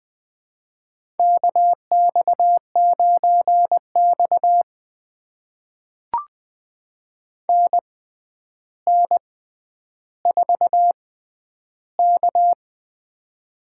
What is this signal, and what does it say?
Morse code, heard as a keyed tone, about 20 wpm, spaced as for 10 wpm, 700 Hz. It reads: KX9X NN4K
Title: USA Call Signs - 4 Characters - 6x Character Spacing - 20wpm